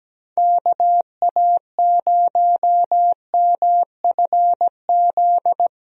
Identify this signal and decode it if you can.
Morse code, heard as a keyed tone, about 17 wpm, 700 Hz. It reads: KA0MFZ
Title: USA Call Signs - 6 Characters - Repeated 3x - 17wpm